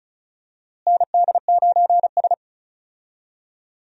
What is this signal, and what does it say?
Morse code, heard as a keyed tone, about 36 wpm, 700 Hz. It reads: ND9S
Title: USA Call Signs - 4 Characters - Repeated 3x - 35wpm